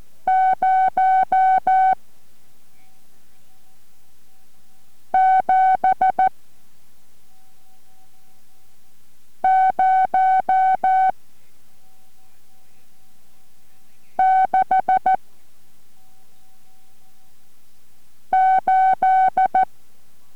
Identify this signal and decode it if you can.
Morse code, heard as a keyed tone, about 14 wpm, 750 Hz. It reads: 0 7 0 6 8